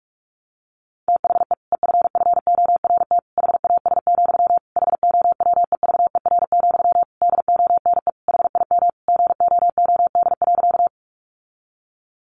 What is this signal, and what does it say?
Morse code, heard as a keyed tone, about 44 wpm, 700 Hz. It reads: THE EFFORT HAS, HOWEVER, DONE HIM GOOD.
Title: Sentences from Top 1200 Words 45wpm